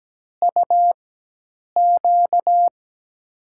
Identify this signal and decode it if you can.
Morse code, heard as a keyed tone, about 17 wpm, spaced as for 8 wpm, 700 Hz. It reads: UQ
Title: USA Call Signs - 6 Characters - 4x Character Spacing - 17wpm